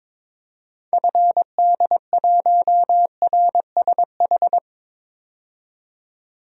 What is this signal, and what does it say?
Morse code, heard as a keyed tone, about 22 wpm, 700 Hz. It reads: FD1RSH
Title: DXCC-100 Call Signs - 6 Characters - 22wpm